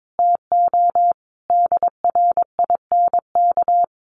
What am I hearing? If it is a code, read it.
Morse code, heard as a keyed tone, about 22 wpm, 700 Hz. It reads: TO DRINK
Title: Sentences from Top 1500 Words 22wpm